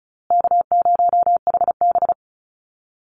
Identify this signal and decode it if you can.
Morse code, heard as a keyed tone, about 34 wpm, 700 Hz. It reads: K0HB
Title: Call Signs Repeated 2x - 35wpm